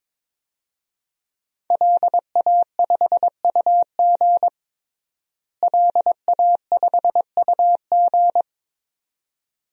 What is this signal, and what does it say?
Morse code, heard as a keyed tone, about 21 wpm, 700 Hz. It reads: LA5UG LA5UG
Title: DXCC-100 Call Signs - 5 Characters - Repeated 2x - 22wpm